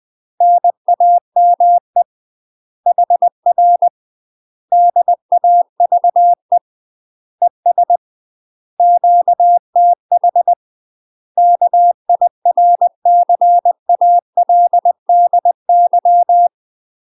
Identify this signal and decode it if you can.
Morse code, heard as a keyed tone, about 20 wpm, 700 Hz. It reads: NAME HR DAVE ES QTH KIRCALDY
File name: UK MASTER QSOs TRAINING / 20WPM / UK MASTER QSOs 0058